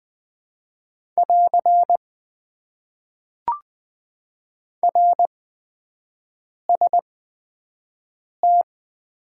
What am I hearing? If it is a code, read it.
Morse code, heard as a keyed tone, about 20 wpm, spaced as for 5 wpm, 700 Hz. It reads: <AR> RST
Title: Top 90 QSO Elements - 8x Character Spacing - 20wpm